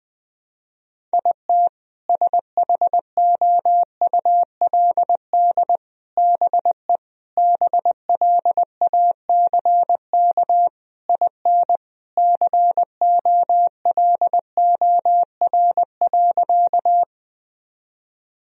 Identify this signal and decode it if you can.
Morse code, heard as a keyed tone, about 20 wpm, 700 Hz. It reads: IT SHOULD BE BLACK IN COLOR.
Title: Sentences from Top 1000 Words - Review Part 2 - 20wpm